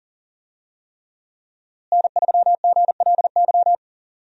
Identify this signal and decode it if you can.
Morse code, heard as a keyed tone, about 40 wpm, 700 Hz. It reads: N3GLY
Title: USA Call Signs - 5 Characters - Repeated 3x - 40wpm